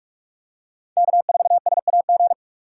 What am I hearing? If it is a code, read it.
Morse code, heard as a keyed tone, about 45 wpm, 700 Hz. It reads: K4SAG
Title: USA Call Signs - 5 Characters - 4x Character Spacing - 45wpm